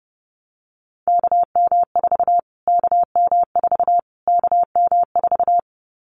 Morse code, heard as a keyed tone, about 30 wpm, 700 Hz. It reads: KM4 KM4 KM4
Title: Callsign Prefixes for Top 25 DXCC - Repeated 3x - 30wpm